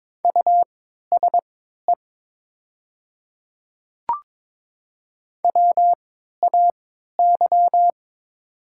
Morse code, heard as a keyed tone, about 22 wpm, spaced as for 13 wpm, 700 Hz. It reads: USE WAY
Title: 3x Character Spacing - Top 100 Words - 22wpm